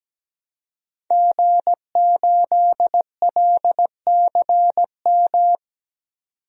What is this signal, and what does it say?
Morse code, heard as a keyed tone, about 17 wpm, 700 Hz. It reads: G8LCM
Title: DXCC-100 Call Signs - 5 Characters - 17wpm